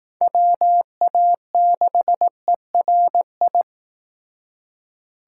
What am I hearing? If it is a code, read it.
Morse code, heard as a keyed tone, about 18 wpm, 700 Hz. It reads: WA6ERI
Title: Speed Racing - USA Call Signs 15wpm